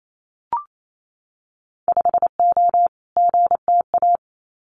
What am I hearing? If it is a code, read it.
Morse code, heard as a keyed tone, about 28 wpm, 690 Hz. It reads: 5O GTA
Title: Winter Field Day Exchanges Encore 28wpm